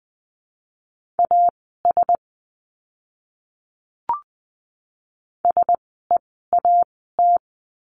Morse code, heard as a keyed tone, about 19 wpm, spaced as for 14 wpm, 700 Hz. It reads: AS SEAT
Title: 068 Lesson 12f - Letter d - Words 20wpm